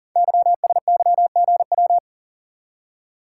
Morse code, heard as a keyed tone, about 38 wpm, 700 Hz. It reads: YSYGW
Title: Code Groups - 5 Letters - 40wpm